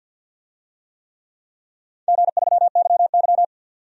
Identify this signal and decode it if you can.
Morse code, heard as a keyed tone, about 45 wpm, 700 Hz. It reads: G3YY
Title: DXCC-100 Call Signs - 4 Characters - 50wpm